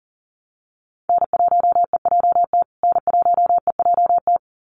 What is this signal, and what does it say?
Morse code, heard as a keyed tone, about 40 wpm, 700 Hz. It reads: N1EJT N1EJT
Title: USA Call Signs Repeated 2x 40wpm